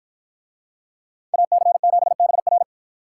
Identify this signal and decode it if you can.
Morse code, heard as a keyed tone, about 53 wpm, 700 Hz. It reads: AC7BR